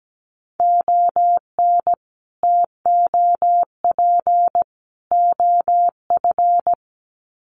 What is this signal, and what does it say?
Morse code, heard as a keyed tone, about 17 wpm, 700 Hz. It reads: ON TOP OF
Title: Sets of 3 Words 17wpm